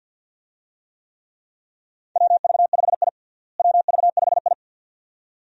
Morse code, heard as a keyed tone, about 52 wpm, 700 Hz. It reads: WV5I WV5I
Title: USA Call Signs - 4 Characters - Repeated 2x - 50wpm